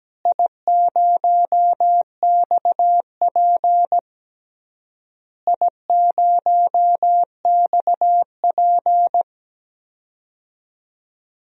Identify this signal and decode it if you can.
Morse code, heard as a keyed tone, about 17 wpm, 700 Hz. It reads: I0XP I0XP
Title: DXCC-100 Call Signs - 4 Characters - Repeated 3x - 17wpm